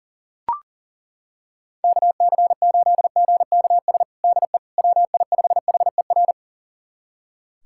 Morse code, heard as a keyed tone, about 40 wpm, 700 Hz. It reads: KC8GKS DE WI5HER
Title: USA Callsign DE Callsign 40wpm